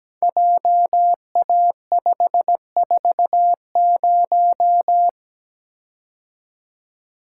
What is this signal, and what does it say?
Morse code, heard as a keyed tone, about 17 wpm, 700 Hz. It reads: JA540
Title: Code Groups - 5 Letter:Numbers - 17wpm